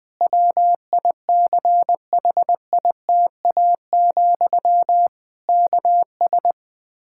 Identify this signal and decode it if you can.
Morse code, heard as a keyed tone, about 20 wpm, 700 Hz. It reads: WICHITA, KS
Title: K1USN SST QSO Practice 20wpm